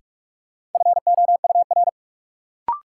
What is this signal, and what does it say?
Morse code, heard as a keyed tone, about 45 wpm, 700 Hz. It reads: FOUR
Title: Sentences from Top 300 Words - New Word Drill - 45wpm